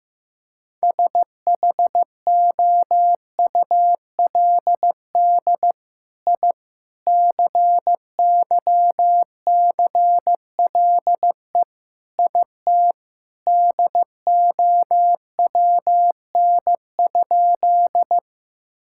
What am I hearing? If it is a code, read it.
Morse code, heard as a keyed tone, about 15 wpm, 700 Hz. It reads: SHOULD I CYCLE IT DOWN?
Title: Sentences from Top 1000 Words - Review Part 6 - 15wpm